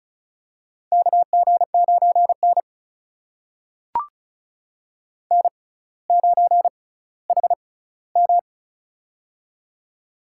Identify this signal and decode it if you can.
Morse code, heard as a keyed tone, about 36 wpm, spaced as for 18 wpm, 700 Hz. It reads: KG9N N9HM
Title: USA Call Signs - 6x Character Spacing - 35wpm